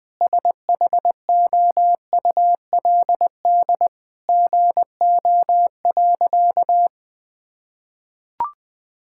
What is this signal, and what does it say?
Morse code, heard as a keyed tone, about 19 wpm, 700 Hz. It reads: SHOULD GO.
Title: Sentences with 4-Word Phrases - 20wpm